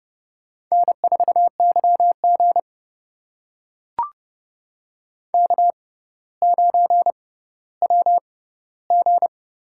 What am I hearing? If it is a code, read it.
Morse code, heard as a keyed tone, about 30 wpm, spaced as for 15 wpm, 700 Hz. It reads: N4YG K9WG
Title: USA Call Signs - 4 Characters - 6x Character Spacing - 30wpm